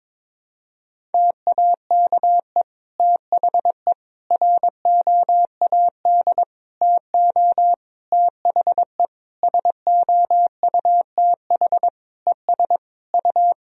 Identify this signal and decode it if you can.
Morse code, heard as a keyed tone, about 23 wpm, 700 Hz. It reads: TAKE THE ROAD TO THE SOUTH ES U
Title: Sentences from Top 400 Words and Abbreviations 22wpm